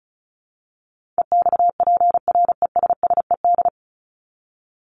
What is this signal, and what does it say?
Morse code, heard as a keyed tone, about 34 wpm, 700 Hz. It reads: EXPRESSED
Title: ED Suffix - Top 100 Words with Base Words 35wpm